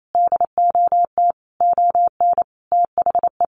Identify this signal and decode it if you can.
Morse code, heard as a keyed tone, about 28 wpm, 700 Hz. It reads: DOT ON THE
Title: Sentences from Top 1500 Words 28wpm